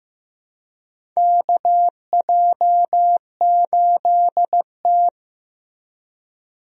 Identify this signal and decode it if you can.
Morse code, heard as a keyed tone, about 15 wpm, 700 Hz. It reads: KJ8T
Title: USA Call Signs - 4 Characters - Repeated 2x - 15wpm